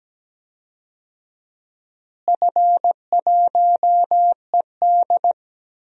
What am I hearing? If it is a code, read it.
Morse code, heard as a keyed tone, about 17 wpm, 700 Hz. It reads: F1ED
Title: DXCC-100 Call Signs - 4 Characters - Repeated 3x - 17wpm